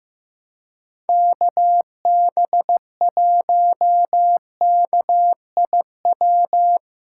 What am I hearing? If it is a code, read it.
Morse code, heard as a keyed tone, about 15 wpm, 700 Hz. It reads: KB1KIW
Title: USA Callsign DE Callsign 15wpm